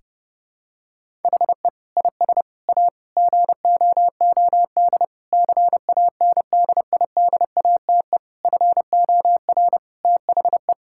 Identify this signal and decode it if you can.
Morse code, heard as a keyed tone, about 30 wpm, 700 Hz. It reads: HE IS A GOOD CANDIDATE FOR THE
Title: Sentences from Top 1600 Words 30wpm